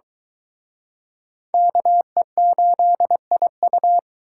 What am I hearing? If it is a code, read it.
Morse code, heard as a keyed tone, about 22 wpm, 700 Hz. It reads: KE8IU